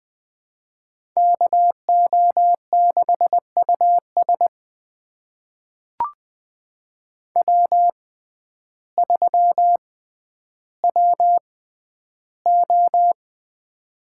Morse code, heard as a keyed tone, about 20 wpm, spaced as for 11 wpm, 700 Hz. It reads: KO6US W3WO
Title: USA Call Signs - 5 Characters - 6x Character Spacing - 20wpm